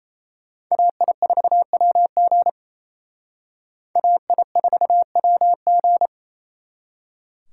Morse code, heard as a keyed tone, about 35 wpm, 700 Hz. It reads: AI4WG AI4WG
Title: Speed Racing - USA Call Signs 28wpm